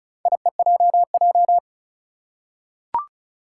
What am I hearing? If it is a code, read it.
Morse code, heard as a keyed tone, about 36 wpm, 700 Hz. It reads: IEJJ
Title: Four Challenging Letter-Number Combinations 35wpm